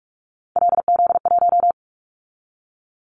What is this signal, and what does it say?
Morse code, heard as a keyed tone, about 44 wpm, 700 Hz. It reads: LZ1
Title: Callsign Prefixes for Top 50 DXCC - 45wpm